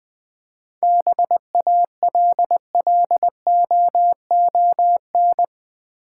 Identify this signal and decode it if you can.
Morse code, heard as a keyed tone, about 20 wpm, 690 Hz. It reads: BALLOON
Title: 7 Letter Words - First 3 Letters Given - 20wpm